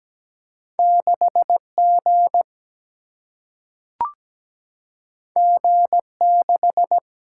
Morse code, heard as a keyed tone, about 17 wpm, 700 Hz. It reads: 6G G6
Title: Single-Letter Single-Number 17wpm